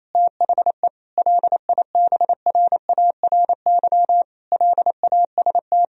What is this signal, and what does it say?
Morse code, heard as a keyed tone, about 28 wpm, 700 Hz. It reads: THE LIBRARY LAST
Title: Sentences from Top 1000 Words - Review Part 4 - 28wpm